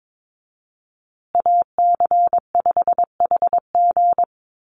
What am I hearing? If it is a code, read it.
Morse code, heard as a keyed tone, about 22 wpm, 700 Hz. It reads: AC5HG
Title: USA Call Signs Repeated 2x 22wpm